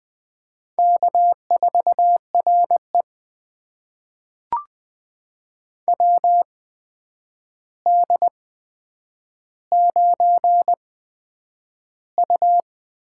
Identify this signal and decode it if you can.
Morse code, heard as a keyed tone, about 20 wpm, spaced as for 9 wpm, 700 Hz. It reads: K4RE WD9U
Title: USA Call Signs - 4 Characters - 8x Character Spacing - 20wpm